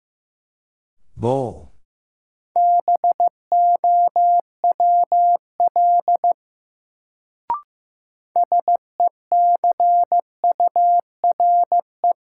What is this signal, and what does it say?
Morse code, heard as a keyed tone, about 15 wpm, 700 Hz. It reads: BOWL SECURE
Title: Sentences from Top 1200 Words - New Word Drill Alt - 15wpm